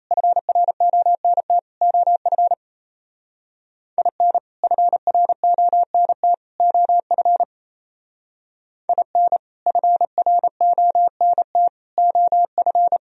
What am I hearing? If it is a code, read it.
Morse code, heard as a keyed tone, about 36 wpm, 700 Hz. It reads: FRONT OF IN FRONT OF IN FRONT OF